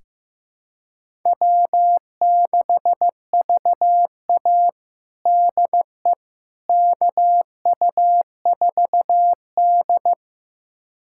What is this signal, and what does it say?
Morse code, heard as a keyed tone, about 15 wpm, 700 Hz. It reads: W6VA DE KU4D
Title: USA Callsign DE Callsign - 4 Characters - 15wpm